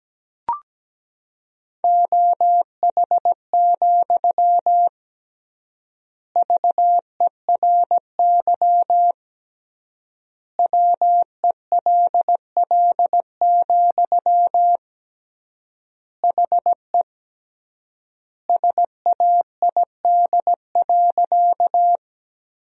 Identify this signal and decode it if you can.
Morse code, heard as a keyed tone, about 17 wpm, 700 Hz. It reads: OH, VERY WELL, HE SAID.
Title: 3x Word-Spacing - Sentences from Top 500 Words - 17wpm